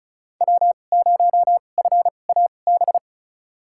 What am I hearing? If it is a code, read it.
Morse code, heard as a keyed tone, about 36 wpm, spaced as for 25 wpm, 700 Hz. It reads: W0FAB
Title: USA Call Signs - 5 Characters - 2x Character Spacing - 35wpm